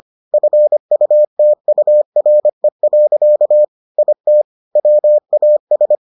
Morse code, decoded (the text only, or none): FUTURE. IT WAS